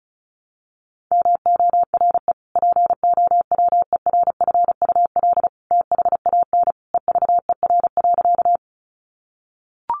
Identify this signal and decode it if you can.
Morse code, heard as a keyed tone, about 34 wpm, 700 Hz. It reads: MORE POWERFUL THAN EVER.
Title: Sentences from Top 1300 Words 35wpm